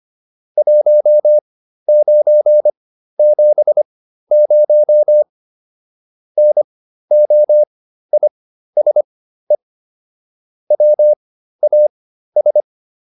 Morse code, heard as a keyed tone, about 25 wpm, spaced as for 13 wpm, 600 Hz. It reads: 1970 NOISE WAS